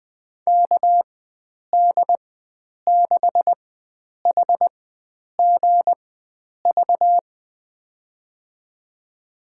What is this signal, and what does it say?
Morse code, heard as a keyed tone, about 21 wpm, spaced as for 9 wpm, 700 Hz. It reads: KD6HGV
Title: USA Call Signs - 6 Characters - 4x Character Spacing - 20wpm